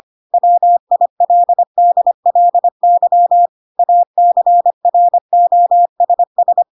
Morse code, heard as a keyed tone, about 25 wpm, 700 Hz. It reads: WILDLY ACROSS
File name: Pinocchio / Pinocchio ch4 25 25 ik1hge